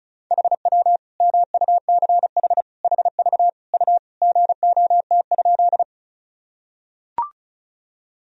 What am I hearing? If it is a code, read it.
Morse code, heard as a keyed tone, about 36 wpm, 700 Hz. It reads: HW MUCH HV U GOT?
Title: Sentences from Top 300 Words and Abbreviations 35wpm